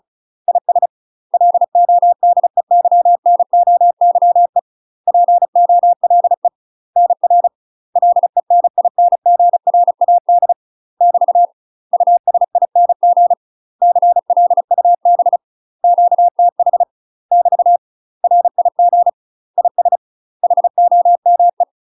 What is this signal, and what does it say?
Morse code, heard as a keyed tone, about 36 wpm, 700 Hz. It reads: IS LODEYNOYE POLE NR LENINGRAD <BT> USING CLUB QTH <BT> RIG IS HOME
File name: UK MASTER QSOs TRAINING / 35WPM / UK MASTER QSOs 0032